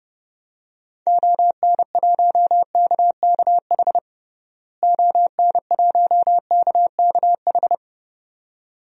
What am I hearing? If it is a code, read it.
Morse code, heard as a keyed tone, about 30 wpm, 700 Hz. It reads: ON1KKH ON1KKH
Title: DXCC-100 Call Signs - 6 Characters - Repeated 2x - 30wpm